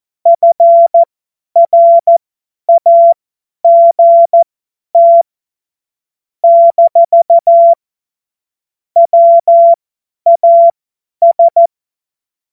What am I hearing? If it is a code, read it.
Morse code, heard as a keyed tone, about 14 wpm, spaced as for 10 wpm, 680 Hz. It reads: FRAGT - WAS
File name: Morsekurs / Witze / Witz 21. September 2021, 50 F70